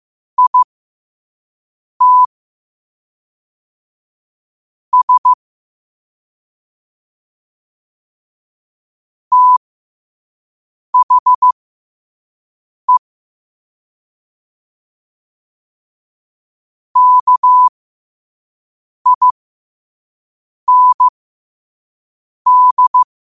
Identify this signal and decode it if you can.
Morse code, heard as a keyed tone, about 15 wpm, spaced as for 5 wpm, 1000 Hz. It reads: IT S THE KIND